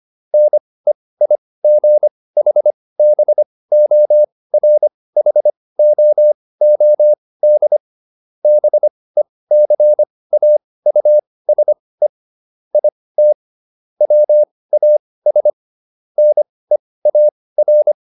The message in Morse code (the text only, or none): NEIGHBORHOOD BECAUSE IT WAS NEAR